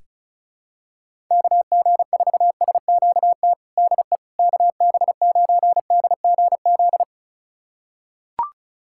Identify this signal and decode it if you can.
Morse code, heard as a keyed tone, about 36 wpm, 690 Hz. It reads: KG4SQT DE KB9DGZ
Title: USA Callsign DE Callsign - 6 Characters - 35wpm